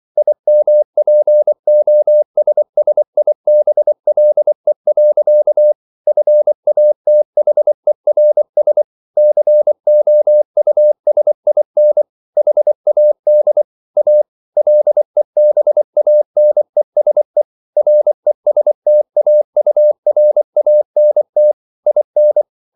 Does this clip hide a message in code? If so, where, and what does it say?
Yes: Morse code from 0.2 to 22.4 s, heard as a keyed tone, about 24 wpm, 600 Hz. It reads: IMPOSSIBLE. FATHERS COUSIN HAD A LEBANESE RESTAURANT IN